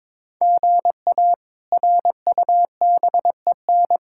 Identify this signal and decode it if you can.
Morse code, heard as a keyed tone, about 22 wpm, 700 Hz. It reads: GA RUBEN
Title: K1USN SST QSO Practice - 22wpm